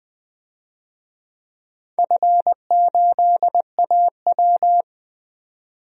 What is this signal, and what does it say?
Morse code, heard as a keyed tone, about 20 wpm, 700 Hz. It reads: F8AW
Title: DXCC-100 Call Signs - 4 Characters - 20wpm